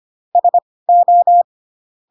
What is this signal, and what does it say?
Morse code, heard as a keyed tone, about 25 wpm, spaced as for 18 wpm, 700 Hz. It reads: SO
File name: Pinocchio / Pinocchio ch1 12 25 ik1hge